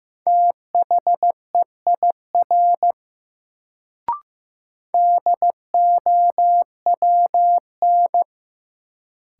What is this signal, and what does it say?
Morse code, heard as a keyed tone, about 15 wpm, 700 Hz. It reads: THEIR DOWN
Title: Sentences from Top 300 Words - New Word Drill - 15wpm